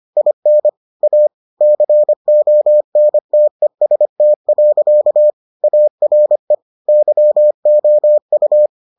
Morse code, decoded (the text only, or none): IN A CONTEST. ARE YOU